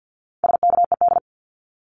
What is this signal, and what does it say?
Morse code, heard as a keyed tone, about 50 wpm, 700 Hz. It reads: SKED
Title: Abbreviations 50wpm